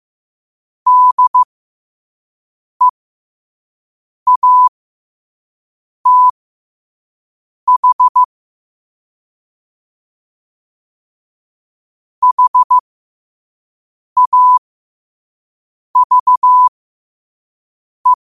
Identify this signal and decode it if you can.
Morse code, heard as a keyed tone, about 15 wpm, spaced as for 5 wpm, 1000 Hz. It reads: DEATH HAVE
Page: New quotes every day in morse code at 5 Words per minute.